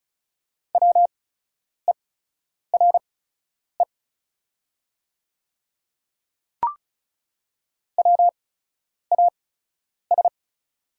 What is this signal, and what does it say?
Morse code, heard as a keyed tone, about 34 wpm, spaced as for 10 wpm, 700 Hz. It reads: WERE WAS
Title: 8x Character Spacing - Top 300 Words - 35wpm